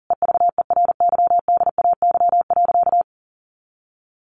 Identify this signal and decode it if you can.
Morse code, heard as a keyed tone, about 40 wpm, 700 Hz. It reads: EVERYDAY.
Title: Additional Easy Phrases 40wpm